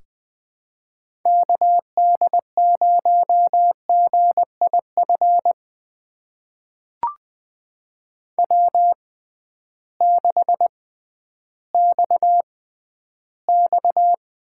Morse code, heard as a keyed tone, about 20 wpm, spaced as for 12 wpm, 700 Hz. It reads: KD0GIF W6XX
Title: USA Call Signs - 6x Character Spacing - 20wpm